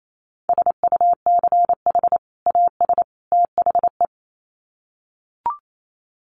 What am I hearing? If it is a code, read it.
Morse code, heard as a keyed tone, about 28 wpm, 700 Hz. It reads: SUCH AS THE
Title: Sets of 3 Words 28wpm